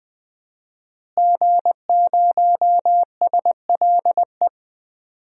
Morse code, heard as a keyed tone, about 20 wpm, 700 Hz. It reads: G0SLE
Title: DXCC-100 Call Signs - 5 Characters - Repeated 2x - 20wpm